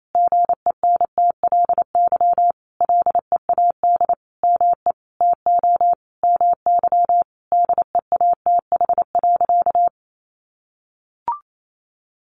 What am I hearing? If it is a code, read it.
Morse code, heard as a keyed tone, about 28 wpm, 700 Hz. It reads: GENTLY LEAD ME TO MY DEATH.
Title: Sentences from Top 1500 Words 28wpm